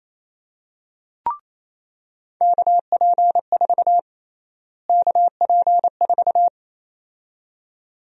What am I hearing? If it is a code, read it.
Morse code, heard as a keyed tone, about 28 wpm, 700 Hz. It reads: KP4 KP4
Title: USA Call Sign Prefixes - Repeated 2x - 28wpm